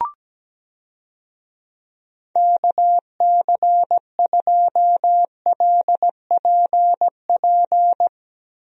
Morse code, heard as a keyed tone, about 17 wpm, 700 Hz. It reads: KC2LPP
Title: USA Call Signs - 6 Characters - Repeated 3x - 17wpm